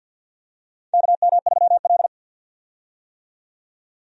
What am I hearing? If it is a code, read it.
Morse code, heard as a keyed tone, about 51 wpm, 700 Hz. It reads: KM3L
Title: USA Call Signs - 4 Characters - 50wpm